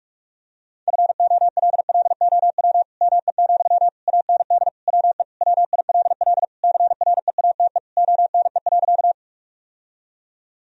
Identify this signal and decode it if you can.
Morse code, heard as a keyed tone, about 41 wpm, 700 Hz. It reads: FOLLOW ME, AND WE WILL CREATE ONE.
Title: Sentences from Top 1000 Words - Review Part 3 - 45wpm